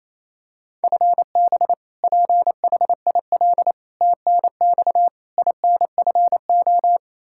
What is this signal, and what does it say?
Morse code, heard as a keyed tone, about 27 wpm, 700 Hz. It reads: FB PHIL TNX INFO
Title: Practice QSO - Exchange 3 - More Info - 28wpm